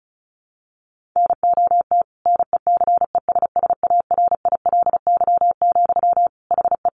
Morse code, heard as a keyed tone, about 34 wpm, 700 Hz. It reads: NOT NECESSARILY, HE
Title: Sentences from Top 1400 Words 35wpm